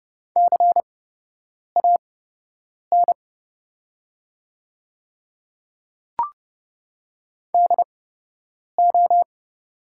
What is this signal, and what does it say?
Morse code, heard as a keyed tone, about 30 wpm, spaced as for 8 wpm, 700 Hz. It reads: CAN DO
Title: 8x Character Spacing - Top 100 Words - 30wpm